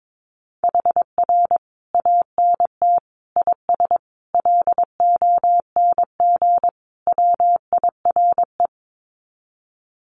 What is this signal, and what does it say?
Morse code, heard as a keyed tone, about 22 wpm, 700 Hz. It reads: HR ANT IS LONG WIRE
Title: Common QSO Phrases 22wpm